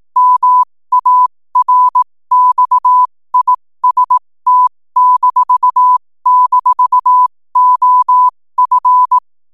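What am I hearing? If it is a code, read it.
Morse code, heard as a keyed tone, about 18 wpm, spaced as for 15 wpm, 1000 Hz. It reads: MARXIST--OF
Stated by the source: 15 wpm Morse code from RNZN Communicators Association.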